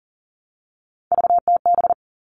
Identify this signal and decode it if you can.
Morse code, heard as a keyed tone, about 40 wpm, 700 Hz. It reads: VTB
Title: All Cut Numbers - 3-Digit 40wpm